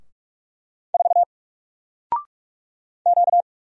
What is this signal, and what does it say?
Morse code, heard as a keyed tone, about 46 wpm, 700 Hz. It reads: 4 Q
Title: Single Letter-Number 45wpm